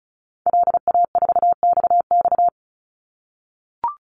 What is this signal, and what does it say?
Morse code, heard as a keyed tone, about 34 wpm, 700 Hz. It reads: LA4XX
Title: Call Signs - 35wpm